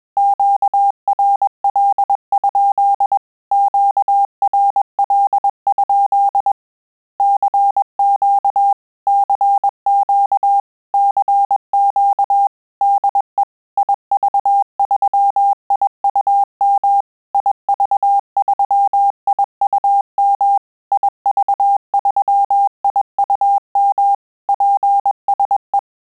Morse code, heard as a keyed tone, about 21 wpm, 800 Hz. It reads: QRL? QRL? CQ CQ CQ DE IV3IUM IV3IUM IV3IUM PSE
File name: QSO CW 100 CPM